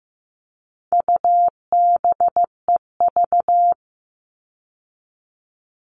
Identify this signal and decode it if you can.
Morse code, heard as a keyed tone, about 15 wpm, 700 Hz. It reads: UBEV
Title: All Cut Numbers - 4-Digit 15wpm